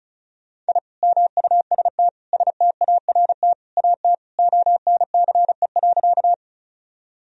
Morse code, heard as a keyed tone, about 34 wpm, 700 Hz. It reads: I MUST START AT ONCE.
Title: Sentences from Top 1000 Words - Review Part 1 - 35wpm